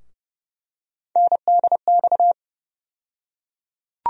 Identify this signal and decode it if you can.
Morse code, heard as a keyed tone, about 30 wpm, 700 Hz. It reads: NDX
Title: USA Call Sign Postfixes - 3 Letters 30wpm